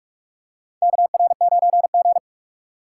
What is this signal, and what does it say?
Morse code, heard as a keyed tone, about 47 wpm, 700 Hz. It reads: KR9G
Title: USA Call Signs - 3x Character Spacing - 45wpm